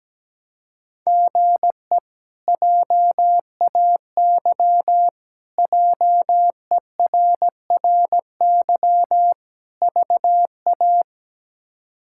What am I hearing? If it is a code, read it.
Morse code, heard as a keyed tone, about 17 wpm, 700 Hz. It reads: GE JAY JERRY VA
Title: K1USN SST QSO Practice - 17wpm